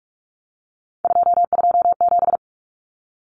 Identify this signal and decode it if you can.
Morse code, heard as a keyed tone, about 44 wpm, 700 Hz. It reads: 227
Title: Numbers - 3-Digit 45wpm